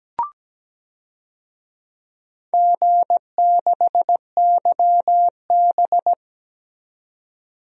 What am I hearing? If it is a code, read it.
Morse code, heard as a keyed tone, about 17 wpm, 700 Hz. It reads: G6YB
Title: DXCC-100 Call Signs - 4 Characters - 17wpm